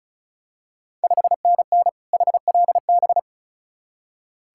Morse code, heard as a keyed tone, about 36 wpm, 690 Hz. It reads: 5NN HLB
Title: Ontario QSO Party Exchange 35wpm